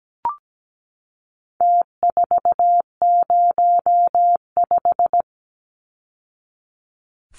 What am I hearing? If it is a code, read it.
Morse code, heard as a keyed tone, about 17 wpm, 700 Hz. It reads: T405
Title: Numbers - Contest Style - 4-Digit 17wpm